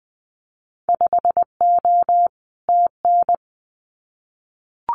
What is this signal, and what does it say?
Morse code, heard as a keyed tone, about 20 wpm, 700 Hz. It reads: 5O TN
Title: Winter Field Day Exchanges Encore 20wpm